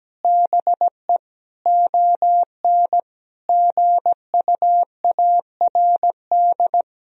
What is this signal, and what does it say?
Morse code, heard as a keyed tone, about 17 wpm, 700 Hz. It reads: BE ON GUARD
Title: Sentences from Top 1000 Words - Review Part 4 - 17wpm